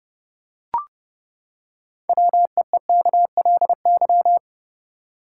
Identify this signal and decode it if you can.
Morse code, heard as a keyed tone, about 31 wpm, 700 Hz. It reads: WEEKLY
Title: Six Letter Words - 30wpm